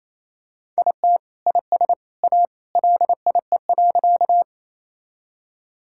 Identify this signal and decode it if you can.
Morse code, heard as a keyed tone, about 28 wpm, 700 Hz. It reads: IT IS A LIE.
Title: Sentences from Top 1600 Words 28wpm